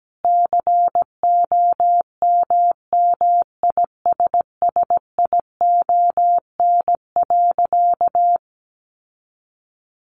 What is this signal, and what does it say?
Morse code, heard as a keyed tone, about 17 wpm, 700 Hz. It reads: COMMISSION.
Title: Sentences from Top 1400 Words 17wpm